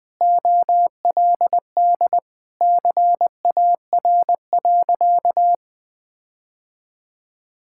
Morse code, heard as a keyed tone, about 20 wpm, 700 Hz. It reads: OLD CAR.
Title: Sentences with 8-Word Phrases - 20wpm